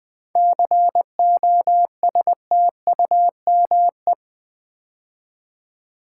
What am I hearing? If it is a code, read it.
Morse code, heard as a keyed tone, about 20 wpm, 700 Hz. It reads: COSTUME
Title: 7 Letter Words - First 3 Letters Given - 20wpm